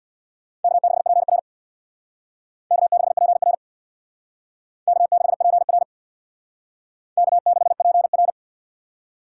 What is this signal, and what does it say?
Morse code, heard as a keyed tone, about 59 wpm, 700 Hz. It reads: K6PR K6PR K6PR K6PR
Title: Speed Racing - USA 1x2 Call Signs 50wpm